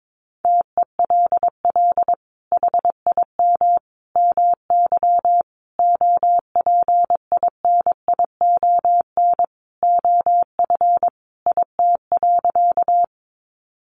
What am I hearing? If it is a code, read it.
Morse code, heard as a keyed tone, about 22 wpm, 700 Hz. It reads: TELL HIM MY OPINION OF IT.
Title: Sentences from Top 1200 Words 22wpm